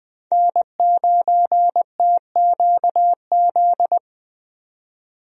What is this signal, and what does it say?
Morse code, heard as a keyed tone, about 20 wpm, 700 Hz. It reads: N9TQZ
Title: USA Call Signs - 5 Characters - 20wpm